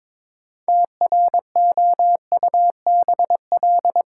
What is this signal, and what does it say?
Morse code, heard as a keyed tone, about 22 wpm, 700 Hz. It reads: TROUBL
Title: Sentences from Top 1200 Words - New Word Drill - 22wpm